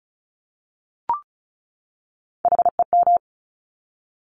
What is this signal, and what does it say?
Morse code, heard as a keyed tone, about 36 wpm, 700 Hz. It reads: HEM
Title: Top Three Letter Combinations in Words 35wpm